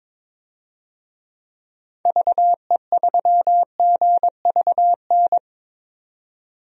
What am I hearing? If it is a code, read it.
Morse code, heard as a keyed tone, about 21 wpm, 700 Hz. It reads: VE3GVN
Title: DXCC-100 Call Signs - 6 Characters - 22wpm